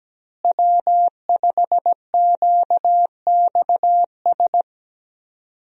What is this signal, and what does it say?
Morse code, heard as a keyed tone, about 17 wpm, 700 Hz. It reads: W5QXS
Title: USA Call Signs - 5 Characters - Repeated 3x - 17wpm